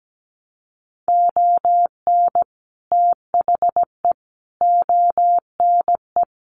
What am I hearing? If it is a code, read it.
Morse code, heard as a keyed tone, about 17 wpm, 700 Hz. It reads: ON THE ONE
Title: Sets of 5 Words Encore2 17wpm